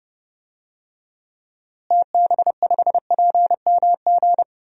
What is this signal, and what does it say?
Morse code, heard as a keyed tone, about 30 wpm, 700 Hz. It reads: TB5PMG
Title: DXCC-100 Call Signs - 6 Characters - Repeated 3x - 30wpm